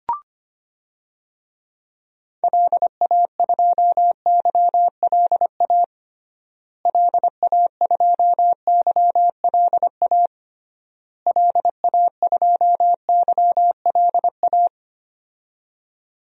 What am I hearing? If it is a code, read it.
Morse code, heard as a keyed tone, about 25 wpm, 700 Hz. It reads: LA2YLA LA2YLA LA2YLA
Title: DXCC-100 Call Signs - 6 Characters - Repeated 3x - 25wpm